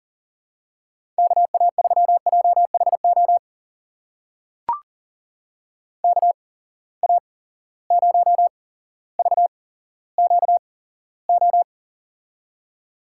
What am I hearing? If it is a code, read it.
Morse code, heard as a keyed tone, about 40 wpm, spaced as for 17 wpm, 700 Hz. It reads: KA3JHO KA0VQO
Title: USA Call Signs - 6 Characters - 8x Character Spacing - 40wpm